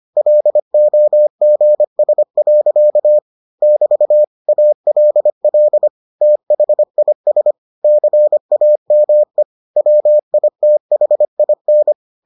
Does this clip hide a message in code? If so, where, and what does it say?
Yes: Morse code from 0.2 to 11.9 s, heard as a keyed tone, about 25 wpm, 600 Hz. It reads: LOGS. <BT> ALL THIS CAME WITHIN